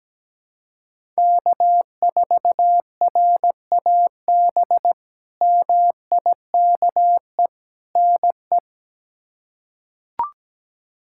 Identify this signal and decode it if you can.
Morse code, heard as a keyed tone, about 17 wpm, 700 Hz. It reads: K4RAB MIKE NE
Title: K1USN SST QSO Practice - 17wpm